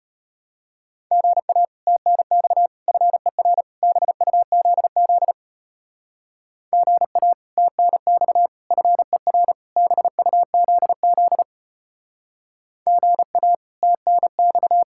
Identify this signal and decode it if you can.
Morse code, heard as a keyed tone, about 33 wpm, 700 Hz. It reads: GA TNX FER BUZZ GA TNX FER BUZZ GA TNX